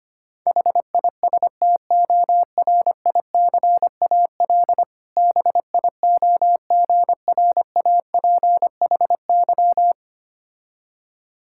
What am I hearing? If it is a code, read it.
Morse code, heard as a keyed tone, about 25 wpm, 700 Hz. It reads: HISTORICAL BIOGRAPHY
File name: practice / 25wpm / Difficult Words with Hints - 25wpm